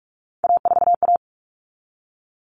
Morse code, heard as a keyed tone, about 44 wpm, 700 Hz. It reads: A4A
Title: All Cut Numbers - 3-Digit 45wpm